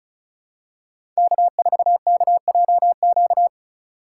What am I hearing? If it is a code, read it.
Morse code, heard as a keyed tone, about 34 wpm, 690 Hz. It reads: K4KJQ
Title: USA Call Signs - 5 Characters - 3x Character Spacing - 35wpm